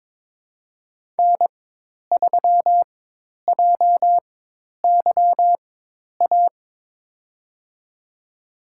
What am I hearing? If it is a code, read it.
Morse code, heard as a keyed tone, about 22 wpm, spaced as for 10 wpm, 700 Hz. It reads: N3JYA
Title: USA Call Signs - 5 Characters - 4x Character Spacing - 22wpm